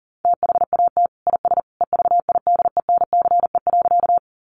Morse code, heard as a keyed tone, about 37 wpm, 700 Hz. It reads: THAT IS EVIDENCE.
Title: Sentences from Top 1300 Words 40wpm